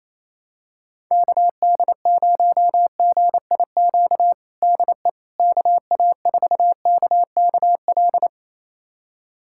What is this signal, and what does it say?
Morse code, heard as a keyed tone, about 28 wpm, 700 Hz. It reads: KD0GIQ DE KA4KKL
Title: USA Callsign DE Callsign - 6 Characters - 28wpm